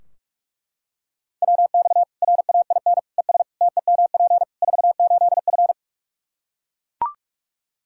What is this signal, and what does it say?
Morse code, heard as a keyed tone, about 41 wpm, 700 Hz. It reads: WX RAIN ES TEMP 48F
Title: Practice QSO - Exchange 2 - Info - 45wpm